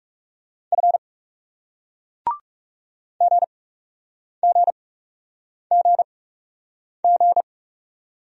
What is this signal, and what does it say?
Morse code, heard as a keyed tone, about 41 wpm, spaced as for 8 wpm, 700 Hz. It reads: F GGGG